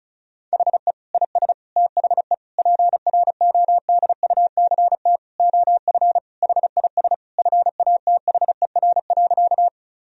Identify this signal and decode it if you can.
Morse code, heard as a keyed tone, about 34 wpm, 700 Hz. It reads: HE IS THE PRODUCT OF HIS FATHER.
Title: Sentences from Top 1000 Words - Review Part 2 - 35wpm